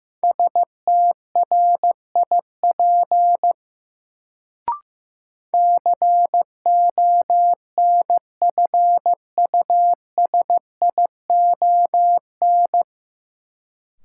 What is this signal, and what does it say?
Morse code, heard as a keyed tone, about 16 wpm, 700 Hz. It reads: STRIP CONFUSION
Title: Sentences from Top 1600 Words - New Word Drill Alt - 15wpm